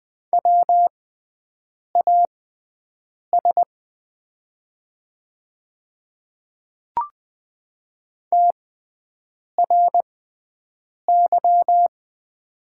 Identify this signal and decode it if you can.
Morse code, heard as a keyed tone, about 20 wpm, spaced as for 7 wpm, 700 Hz. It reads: WAS TRY
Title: 6x Character Spacing - Top 200 Words - 20wpm